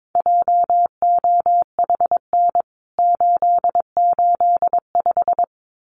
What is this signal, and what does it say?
Morse code, heard as a keyed tone, about 22 wpm, 700 Hz. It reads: JOHN 885
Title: CWT Exchange 22wpm